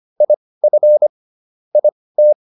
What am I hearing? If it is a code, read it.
Morse code, heard as a keyed tone, about 25 wpm, spaced as for 18 wpm, 600 Hz. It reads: IF IT